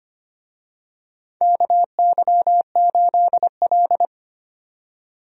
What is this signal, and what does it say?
Morse code, heard as a keyed tone, about 25 wpm, 700 Hz. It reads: KY8L
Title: USA Call Signs - 4 Characters - 25wpm